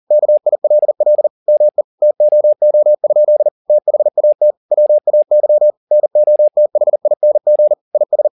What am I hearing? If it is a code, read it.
Morse code, heard as a keyed tone, about 40 wpm, 600 Hz. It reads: KILL ME TOO? THAT WAY NOTHING IS
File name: the old man and the sea 2 40wpm